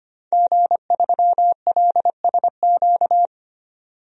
Morse code, heard as a keyed tone, about 25 wpm, 700 Hz. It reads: G3LSQ
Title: DXCC-100 Call Signs - 5 Characters - 25wpm